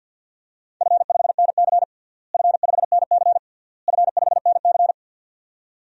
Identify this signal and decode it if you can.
Morse code, heard as a keyed tone, about 50 wpm, 700 Hz. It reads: F5NC F5NC F5NC
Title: DXCC-100 Call Signs - 4 Characters - Repeated 3x - 50wpm